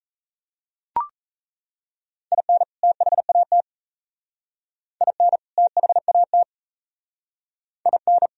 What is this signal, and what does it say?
Morse code, heard as a keyed tone, about 34 wpm, 700 Hz. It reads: IN THAT IN THAT IN
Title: Speed-Racing - Sets of 2 Words 28wpm